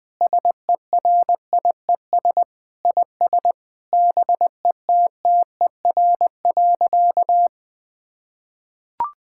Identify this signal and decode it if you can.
Morse code, heard as a keyed tone, about 20 wpm, 700 Hz. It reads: SERIES IS BETTER.
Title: Sentences from Top 1000 Words - Review Part 5 - 20wpm